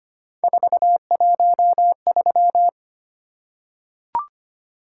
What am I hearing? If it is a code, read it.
Morse code, heard as a keyed tone, about 25 wpm, 700 Hz. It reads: 413
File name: Three Letter-Number Combinations - 25wpm